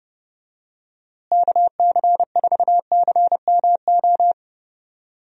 Morse code, heard as a keyed tone, about 30 wpm, 700 Hz. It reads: KC4CMO
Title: USA Call Signs - 6 Characters - Repeated 3x - 30wpm